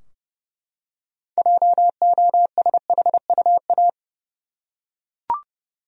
Morse code, heard as a keyed tone, about 30 wpm, 700 Hz. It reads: JOSHUA
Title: USA Common Names - 30wpm